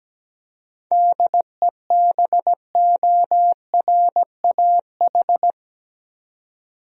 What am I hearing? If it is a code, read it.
Morse code, heard as a keyed tone, about 17 wpm, 700 Hz. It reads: DEBORAH
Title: USA Common Names - 17wpm